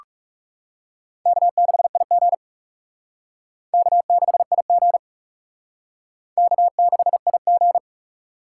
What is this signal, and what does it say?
Morse code, heard as a keyed tone, about 40 wpm, 700 Hz. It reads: K6IG K6IG K6IG